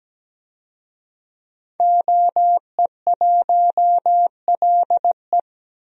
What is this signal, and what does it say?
Morse code, heard as a keyed tone, about 17 wpm, 700 Hz. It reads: OE1LE
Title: DXCC-100 Call Signs - 5 Characters - Repeated 3x - 17wpm